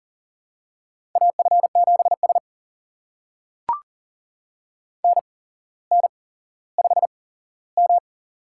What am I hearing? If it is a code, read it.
Morse code, heard as a keyed tone, about 38 wpm, spaced as for 17 wpm, 700 Hz. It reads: AF7S NN5M
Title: USA Call Signs - 4 Characters - 8x Character Spacing - 40wpm